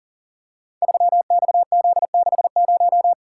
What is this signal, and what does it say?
Morse code, heard as a keyed tone, about 42 wpm, 690 Hz. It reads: 3XZ60
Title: Code Groups - 5 Letter:Numbers - 40wpm